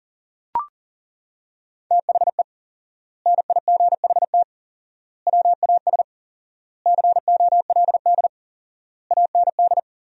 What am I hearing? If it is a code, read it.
Morse code, heard as a keyed tone, about 40 wpm, 700 Hz. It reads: THE NIGHT WAS COLD AND
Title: 4x Word-Spacing - Sentences from Top 400 Words - 40wpm